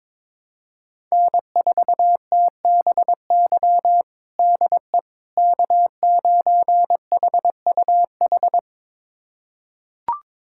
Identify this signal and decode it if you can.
Morse code, heard as a keyed tone, about 22 wpm, 700 Hz. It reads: N4TBY DE K9HUH
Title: USA Callsign DE Callsign - 5 Characters - 22wpm